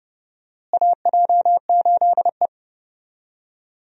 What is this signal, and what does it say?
Morse code, heard as a keyed tone, about 32 wpm, 700 Hz. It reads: AJ8E
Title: USA Call Signs - 4 Characters - Repeated 2x - 30wpm